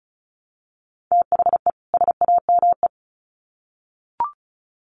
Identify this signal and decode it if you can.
Morse code, heard as a keyed tone, about 34 wpm, 700 Hz. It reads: THE SAME
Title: Sets of 2 Words 35wpm